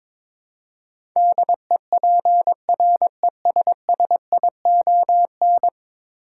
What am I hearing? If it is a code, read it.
Morse code, heard as a keyed tone, about 22 wpm, 700 Hz. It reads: DEPRESSION
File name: practice / 22wpm / 10 Letter Words - First 3 Letters Given - 22wpm